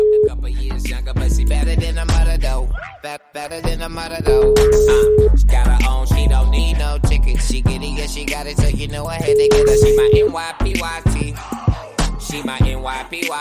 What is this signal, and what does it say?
Ringback tone (Europe, 425 Hz), with speech.